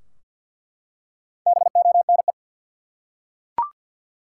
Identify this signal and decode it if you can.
Morse code, heard as a keyed tone, about 48 wpm, 700 Hz. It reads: BONE
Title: Four Letter Words - 50wpm